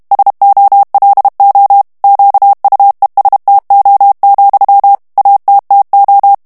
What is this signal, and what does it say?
Morse code, heard as a keyed tone, about 32 wpm, 800 Hz. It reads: SOLO QUESTO, ATTO
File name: ITALIEN texte clair 2 vitesse 30 mots minute